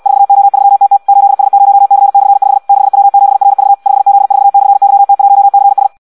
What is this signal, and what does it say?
Morse code, heard as a keyed tone, about 74 wpm, 800 Hz. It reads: 4O3TT 8S0C/5 6W7RV 5Z4/YT1CS